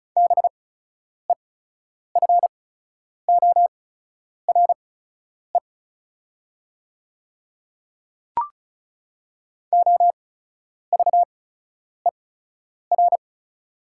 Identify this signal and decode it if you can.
Morse code, heard as a keyed tone, about 36 wpm, spaced as for 10 wpm, 690 Hz. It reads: BEFORE OVER
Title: 8x Character Spacing - Top 200 Words - 35wpm